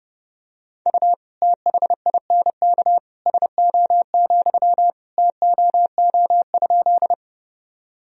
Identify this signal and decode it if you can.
Morse code, heard as a keyed tone, about 30 wpm, 700 Hz. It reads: U THINK SO, TOO?
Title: Sentences from Top 200 Words and Abbreviations 30wpm